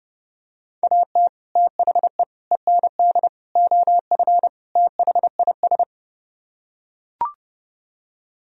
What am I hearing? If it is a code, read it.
Morse code, heard as a keyed tone, about 29 wpm, 700 Hz. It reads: AT THE END OF THIS